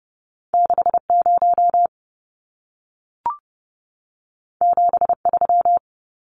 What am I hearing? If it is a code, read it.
Morse code, heard as a keyed tone, about 30 wpm, 700 Hz. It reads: 60 73
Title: Numbers - 2-Digit 30wpm